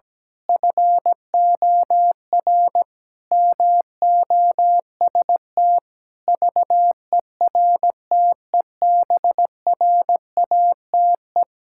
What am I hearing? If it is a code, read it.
Morse code, heard as a keyed tone, about 17 wpm, 700 Hz. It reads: FOR MOST VERTEBRATE
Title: Pangram Challenge v2 - 17wpm